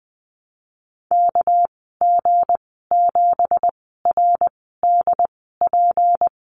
Morse code, heard as a keyed tone, about 20 wpm, spaced as for 14 wpm, 700 Hz. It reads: KG7RDP
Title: 245 Lesson 39g - Letter z - Call Signs 20wpm